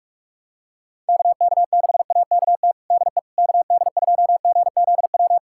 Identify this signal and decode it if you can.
Morse code, heard as a keyed tone, about 46 wpm, 700 Hz. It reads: KK6AKT DE KD2GZW
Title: USA Callsign DE Callsign - 6 Characters - 45wpm